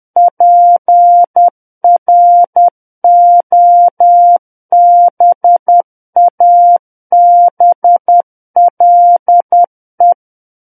Morse code, heard as a keyed tone, about 10 wpm, 700 Hz. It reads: PROBABLE
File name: Book One / 01-Chapter 1 - 10 WPM